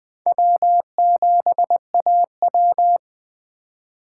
Morse code, heard as a keyed tone, about 22 wpm, 700 Hz. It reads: W7AW